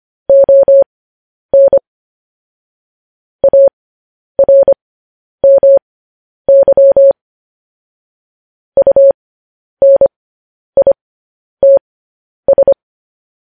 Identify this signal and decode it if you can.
Morse code, heard as a keyed tone, about 25 wpm, spaced as for 10 wpm, 550 Hz. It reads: ON ARMY UNITS
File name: podcast / News headlines-1x-25x10